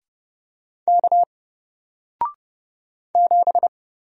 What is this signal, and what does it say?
Morse code, heard as a keyed tone, about 31 wpm, 700 Hz. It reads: K 7
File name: Single-Letter-Number-30wpm